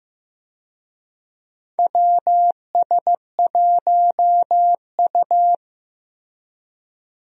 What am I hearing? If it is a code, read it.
Morse code, heard as a keyed tone, about 15 wpm, 700 Hz. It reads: WS1U
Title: USA Call Signs - 4 Characters - Repeated 2x - 15wpm